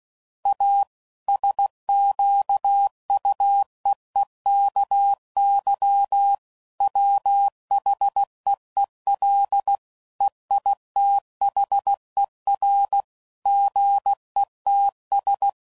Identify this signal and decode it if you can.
Morse code, heard as a keyed tone, about 16 wpm, 800 Hz. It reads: A SQUEEKY WHEEL EITHER GETS
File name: Sayings / Chuckles / Squeaky